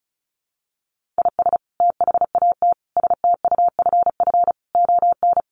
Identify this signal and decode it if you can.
Morse code, heard as a keyed tone, about 34 wpm, 700 Hz. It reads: IS THAT STUFF ON
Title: Sentences from Top 1400 Words 35wpm